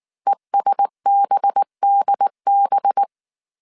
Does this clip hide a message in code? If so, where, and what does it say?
Yes: Morse code from 0.3 to 3.0 s, heard as a keyed tone, about 19 wpm, 800 Hz. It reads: ESBDB